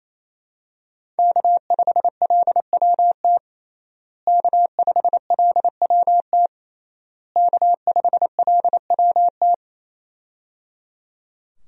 Morse code, heard as a keyed tone, about 28 wpm, 700 Hz. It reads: K5LWT K5LWT K5LWT
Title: USA Call Signs - 5 Characters - Repeated 3x - 28wpm